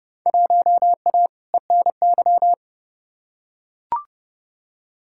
Morse code, heard as a keyed tone, about 30 wpm, 700 Hz. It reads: 1A ENY
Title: Field Day Exchanges - 30wpm